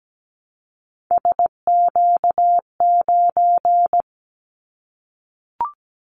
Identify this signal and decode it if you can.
Morse code, heard as a keyed tone, about 17 wpm, 700 Hz. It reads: SQ9
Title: Callsign Prefixes for Top 25 DXCC - Repeated 3x - 17wpm